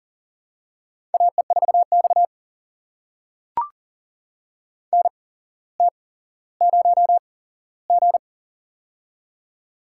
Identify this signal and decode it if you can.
Morse code, heard as a keyed tone, about 44 wpm, spaced as for 17 wpm, 700 Hz. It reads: AE4X NT0G
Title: USA Call Signs - 4 Characters - 8x Character Spacing - 40wpm